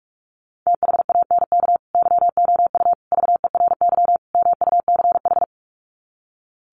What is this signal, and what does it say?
Morse code, heard as a keyed tone, about 44 wpm, 700 Hz. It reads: THANK YOU VERY MUCH
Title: Sets of 4 Words 45wpm